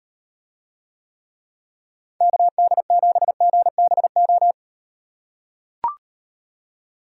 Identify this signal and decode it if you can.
Morse code, heard as a keyed tone, about 38 wpm, 700 Hz. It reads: KD7GBO